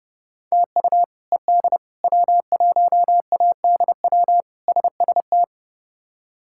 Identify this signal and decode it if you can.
Morse code, heard as a keyed tone, about 30 wpm, 700 Hz. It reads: TU ED W1ADW SST
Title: K1USN SST QSO Practice - 30wpm